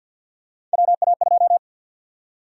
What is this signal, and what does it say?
Morse code, heard as a keyed tone, about 51 wpm, 700 Hz. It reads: WA2
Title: USA Call Sign Prefixes - 50wpm